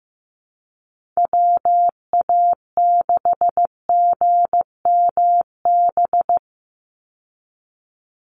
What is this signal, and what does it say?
Morse code, heard as a keyed tone, about 15 wpm, 700 Hz. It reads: WA6GMB
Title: USA Call Signs 15wpm